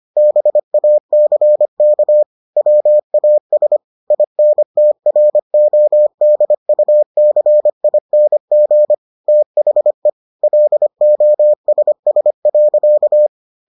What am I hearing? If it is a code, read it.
Morse code, heard as a keyed tone, about 25 wpm, 600 Hz. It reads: BACK WAS INTRODUCING THE LOSS.